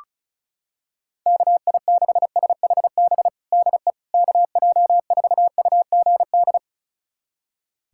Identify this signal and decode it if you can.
Morse code, heard as a keyed tone, about 36 wpm, 700 Hz. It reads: KI6SHB DE KJ4UGD
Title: USA Callsign DE Callsign 35wpm